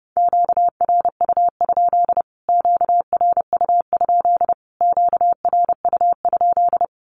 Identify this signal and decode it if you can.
Morse code, heard as a keyed tone, about 28 wpm, 700 Hz. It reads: QRU? QRU? QRU?
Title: Q-Codes Repeated 3x 30wpm